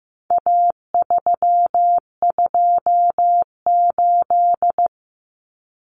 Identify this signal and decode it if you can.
Morse code, heard as a keyed tone, about 15 wpm, 700 Hz. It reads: A328
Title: Common Cut Numbers - 4-Digit 15wpm